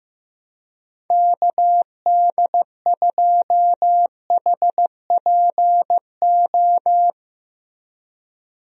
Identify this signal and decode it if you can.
Morse code, heard as a keyed tone, about 15 wpm, 700 Hz. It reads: KD2HPO
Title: USA Call Signs - 6 Characters - Repeated 2x - 15wpm